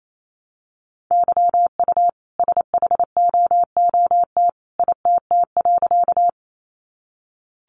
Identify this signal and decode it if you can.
Morse code, heard as a keyed tone, about 28 wpm, 700 Hz. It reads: YU SHOOT ITT.
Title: A Sound Of Thunder 28wpm